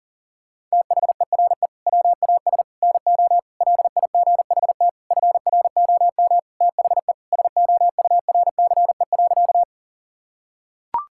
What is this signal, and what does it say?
Morse code, heard as a keyed tone, about 37 wpm, 700 Hz. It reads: THERE WAS NO LIGHT FROM THE SOURCE.
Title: Sentences from Top 1000 Words - Review Part 5 - 40wpm